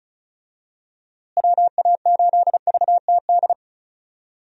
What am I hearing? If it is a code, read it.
Morse code, heard as a keyed tone, about 34 wpm, 700 Hz. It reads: WA8VTD
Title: USA Call Signs - 6 Characters - Repeated 2x - 35wpm